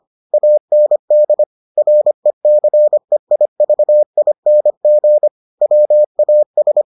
Morse code, heard as a keyed tone, about 25 wpm, 600 Hz. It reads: AND RECEIVING WAS